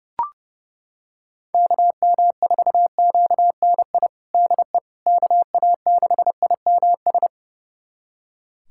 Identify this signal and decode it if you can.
Morse code, heard as a keyed tone, about 30 wpm, 700 Hz. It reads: KM4QNI DE KA6IMS
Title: USA Callsign DE Callsign - 6 Characters - 30wpm